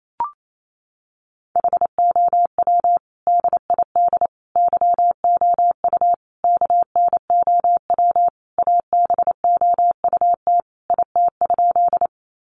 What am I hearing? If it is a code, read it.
Morse code, heard as a keyed tone, about 28 wpm, 700 Hz. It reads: HOW DID YOU KNOW ABOUT IT?
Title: Sentences from Top 100 Words 28wpm